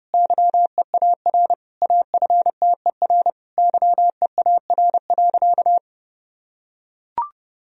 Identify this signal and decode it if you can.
Morse code, heard as a keyed tone, about 30 wpm, 700 Hz. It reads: YEAR AFTER YEAR.
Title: Sentences from Top 1000 Words - Review Part 1 - 30wpm